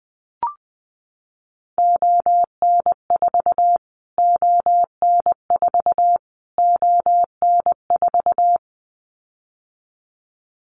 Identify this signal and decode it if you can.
Morse code, heard as a keyed tone, about 20 wpm, 700 Hz. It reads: ON4 ON4 ON4
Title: Callsign Prefixes for Top 25 DXCC - Repeated 3x - 20wpm